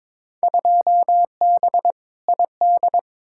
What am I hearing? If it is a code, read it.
Morse code, heard as a keyed tone, about 22 wpm, 700 Hz. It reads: 2B ID
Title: Field Day Exchanges Encore - 22wpm